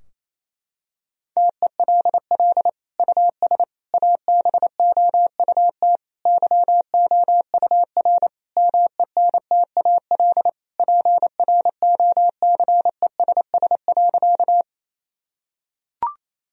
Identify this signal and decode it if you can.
Morse code, heard as a keyed tone, about 28 wpm, 700 Hz. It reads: TELL US ABOUT YOUR MENTAL PROCESS.
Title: Sentences from Top 1000 Words - Review Part 4 - 28wpm